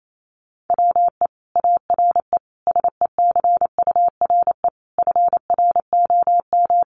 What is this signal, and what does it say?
Morse code, heard as a keyed tone, about 28 wpm, 700 Hz. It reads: WE ARE SECURE FROM
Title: Sentences from Top 1200 Words 28wpm